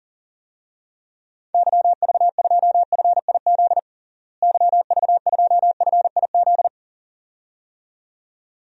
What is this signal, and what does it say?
Morse code, heard as a keyed tone, about 40 wpm, 700 Hz. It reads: YV2FIZ YV2FIZ
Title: DXCC-100 Call Signs - 6 Characters - Repeated 2x - 40wpm